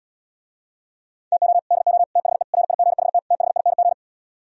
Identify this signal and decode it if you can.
Morse code, heard as a keyed tone, about 77 wpm, 700 Hz. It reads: TO DO THE RIGHT THING